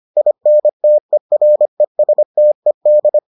Morse code, heard as a keyed tone, about 25 wpm, 600 Hz. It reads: INTERESTED